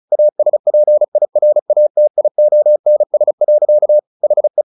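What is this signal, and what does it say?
Morse code, heard as a keyed tone, about 34 wpm, 600 Hz. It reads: ASPIRATIONS. HE
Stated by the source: Morse practice at 35 wpm from CW med Gnister, sending Ken Follett's novel Never in English.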